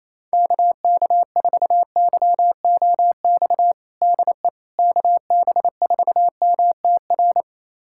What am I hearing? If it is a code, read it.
Morse code, heard as a keyed tone, about 28 wpm, 700 Hz. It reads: KK4YOX DE KB4MTR
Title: USA Callsign DE Callsign 28wpm